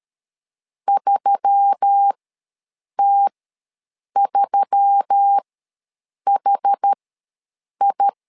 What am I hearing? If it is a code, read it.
Morse code, heard as a keyed tone, about 13 wpm, spaced as for 7 wpm, 800 Hz. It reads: 3T3HI
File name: LEZIONI / 4 Q1S3 ws 12wpm 2038